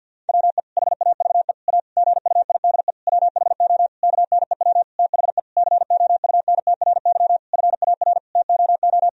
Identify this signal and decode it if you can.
Morse code, heard as a keyed tone, about 50 wpm, 700 Hz. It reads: WE HAVE A GUIDE WHO KNEW THE COUNTRY FAR TOO
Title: Sentences from Top 1000 Words - Review Part 3 - 50wpm